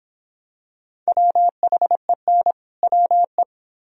Morse code, heard as a keyed tone, about 26 wpm, 700 Hz. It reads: WHEN WE